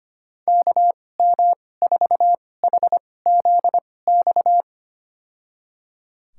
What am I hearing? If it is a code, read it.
Morse code, heard as a keyed tone, about 25 wpm, spaced as for 18 wpm, 700 Hz. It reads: KM4HZX
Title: USA Call Signs - 6 Characters - 2x Character Spacing - 25wpm